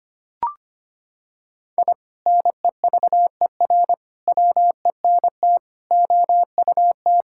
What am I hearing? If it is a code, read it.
Morse code, heard as a keyed tone, about 25 wpm, 700 Hz. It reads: I NEVER WENT OUT
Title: Sentences from Top 1000 Words - Review Part 5 - 25wpm